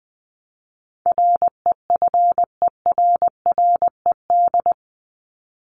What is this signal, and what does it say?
Morse code, heard as a keyed tone, about 20 wpm, 700 Hz. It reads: REFERRED
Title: ED Suffix - Top 100 Words with Base Words 20wpm